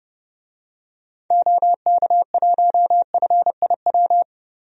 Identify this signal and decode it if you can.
Morse code, heard as a keyed tone, about 30 wpm, 700 Hz. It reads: OK1FIW
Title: DXCC-100 Call Signs - 6 Characters - 30wpm